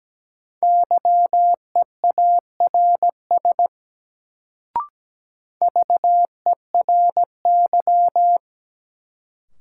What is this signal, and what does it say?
Morse code, heard as a keyed tone, about 17 wpm, 700 Hz. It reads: YEARS VERY
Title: Sentences from Top 200 Words - New Word Drill - 17wpm